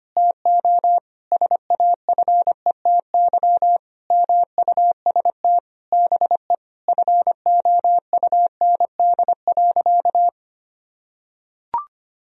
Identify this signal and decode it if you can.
Morse code, heard as a keyed tone, about 25 wpm, 700 Hz. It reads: TO SAFETY MUST BE FOUND.
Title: Sentences from Top 1000 Words - Review Part 4 - 25wpm